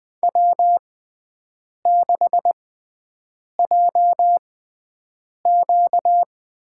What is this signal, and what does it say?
Morse code, heard as a keyed tone, about 20 wpm, spaced as for 7 wpm, 700 Hz. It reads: W6JQ
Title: USA Call Signs - 6x Character Spacing - 20wpm